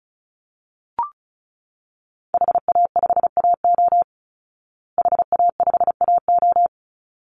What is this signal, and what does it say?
Morse code, heard as a keyed tone, about 34 wpm, 700 Hz. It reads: HA5AO HA5AO
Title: Call Signs Repeated 3x - 35wpm